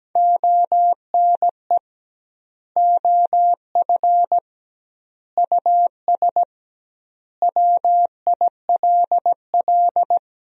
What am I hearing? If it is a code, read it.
Morse code, heard as a keyed tone, about 17 wpm, 700 Hz. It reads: ONE OF US WILL
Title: 2x Word-Spacing - Sentences from Top 300 Words - 17wpm